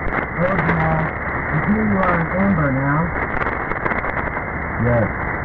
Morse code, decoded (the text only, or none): none